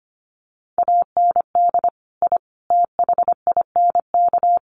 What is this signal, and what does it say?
Morse code, heard as a keyed tone, about 25 wpm, 700 Hz. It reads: AND I THINK
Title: Sets of 4 Words 25wpm